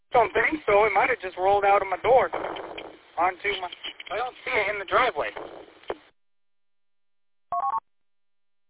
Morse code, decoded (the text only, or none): none